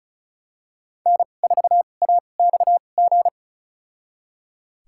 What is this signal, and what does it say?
Morse code, heard as a keyed tone, about 36 wpm, spaced as for 25 wpm, 690 Hz. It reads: N4AXG
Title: USA Call Signs - 5 Characters - 2x Character Spacing - 35wpm